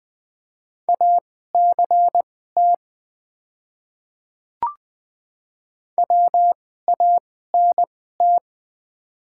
Morse code, heard as a keyed tone, about 20 wpm, spaced as for 14 wpm, 700 Hz. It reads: ACT WANT
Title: 2x Character Spacing - Top 300 Words - 20wpm